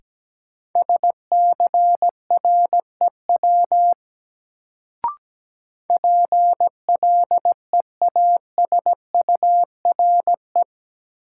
Morse code, heard as a keyed tone, about 17 wpm, 700 Hz. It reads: SCREW PLEASURE
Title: Sentences from Top 1600 Words - New Word Drill Alt - 17wpm